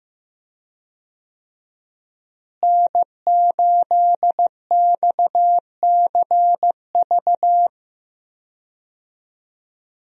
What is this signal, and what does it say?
Morse code, heard as a keyed tone, about 15 wpm, 700 Hz. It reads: N8XCV
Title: USA Call Signs - 5 Characters - Repeated 3x - 15wpm